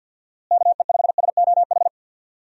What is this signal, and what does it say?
Morse code, heard as a keyed tone, about 50 wpm, 700 Hz. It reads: KE5SOH
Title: USA Call Signs - 6 Characters - Repeated 3x - 50wpm